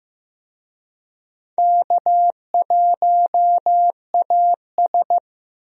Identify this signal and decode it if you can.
Morse code, heard as a keyed tone, about 15 wpm, 700 Hz. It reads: K1AS
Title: USA Call Signs - 4 Characters - Repeated 3x - 15wpm